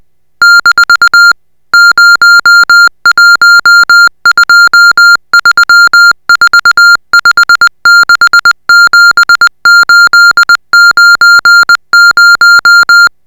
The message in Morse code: - 01234567890